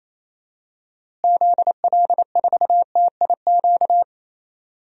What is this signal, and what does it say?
Morse code, heard as a keyed tone, about 27 wpm, 700 Hz. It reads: ZL4TIQ
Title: DXCC-100 Call Signs - 6 Characters - Repeated 3x - 28wpm